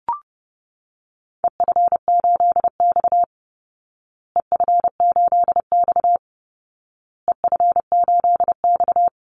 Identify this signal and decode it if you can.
Morse code, heard as a keyed tone, about 30 wpm, 700 Hz. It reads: EF8X EF8X EF8X
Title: Call Signs Repeated 2x - 30wpm